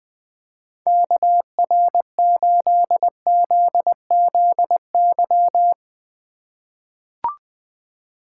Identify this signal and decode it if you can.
Morse code, heard as a keyed tone, about 20 wpm, 700 Hz. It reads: KR8ZZY
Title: USA Call Signs - 3x Character Spacing - 20wpm